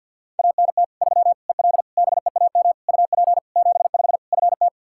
Morse code, heard as a keyed tone, about 52 wpm, 700 Hz. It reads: ANT 3 EL BEAM UP 75 FT
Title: Practice QSO - Exchange 2 - Info - 50wpm